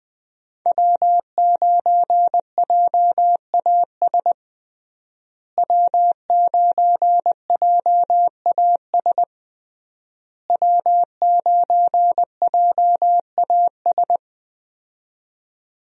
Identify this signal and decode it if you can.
Morse code, heard as a keyed tone, about 21 wpm, 700 Hz. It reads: W9JAS W9JAS W9JAS
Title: USA Call Signs - 5 Characters - Repeated 3x - 20wpm